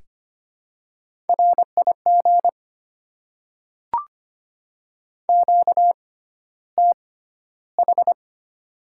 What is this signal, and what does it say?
Morse code, heard as a keyed tone, about 25 wpm, spaced as for 13 wpm, 690 Hz. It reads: RIG QTH
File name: Top 20 QSO Elements - 6x Character Spacing - 25wpm